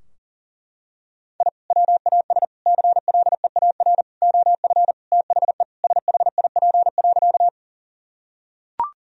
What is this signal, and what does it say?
Morse code, heard as a keyed tone, about 40 wpm, 700 Hz. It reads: I WAS CLEAR OF THE SHIP.
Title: Sentences from Top 1000 Words - Review Part 2 - 40wpm